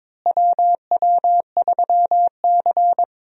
Morse code, heard as a keyed tone, about 22 wpm, 700 Hz. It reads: WW3C
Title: USA Call Signs - 4 Characters - Repeated 3x - 22wpm